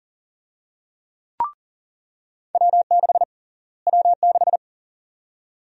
Morse code, heard as a keyed tone, about 40 wpm, 700 Hz. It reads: W6 W6
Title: USA Call Sign Prefixes - Repeated 2x - 40wpm